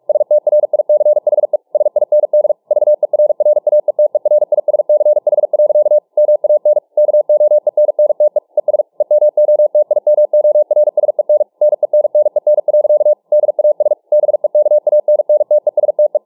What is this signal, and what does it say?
Morse code, heard as a keyed tone, about 48 wpm, 600 Hz. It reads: STRICHE SIND VERRAETERISCH. MAN KOENNTE ES EMOTIMORSEN NENNEN. DAS BEKANNTESTE